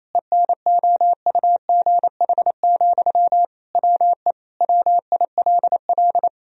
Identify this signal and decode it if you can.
Morse code, heard as a keyed tone, about 28 wpm, 700 Hz. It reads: ENOUGH, WE WILL
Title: Sentences from Top 1000 Words - Review Part 4 - 28wpm